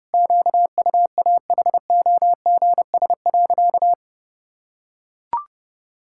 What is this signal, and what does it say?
Morse code, heard as a keyed tone, about 30 wpm, 700 Hz. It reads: QUAHOGS.
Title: Pangram Challenge v2 - 30wpm